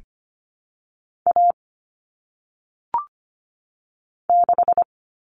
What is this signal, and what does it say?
Morse code, heard as a keyed tone, about 25 wpm, 700 Hz. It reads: A 6
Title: Common Cut Numbers - 1-Digit 25wpm